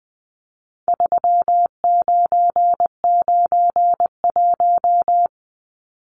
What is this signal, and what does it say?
Morse code, heard as a keyed tone, about 20 wpm, 700 Hz. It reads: 3991
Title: Numbers - 4-Digit 20wpm